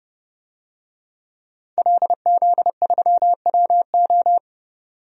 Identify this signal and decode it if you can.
Morse code, heard as a keyed tone, about 30 wpm, 700 Hz. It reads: LZ3WO
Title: DXCC-100 Call Signs - 5 Characters - Repeated 2x - 30wpm